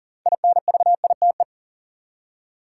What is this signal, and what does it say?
Morse code, heard as a keyed tone, about 40 wpm, 700 Hz. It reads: INVITE
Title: Six Letter Words - 40wpm